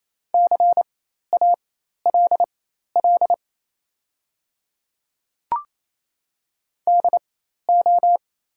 Morse code, heard as a keyed tone, about 28 wpm, spaced as for 13 wpm, 700 Hz. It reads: CALL DO
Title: 4x Character Spacing - Top 100 Words - 28wpm